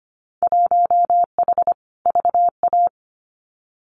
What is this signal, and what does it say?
Morse code, heard as a keyed tone, about 25 wpm, 690 Hz. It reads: 1H VA
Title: Winter Field Day Exchanges 25wpm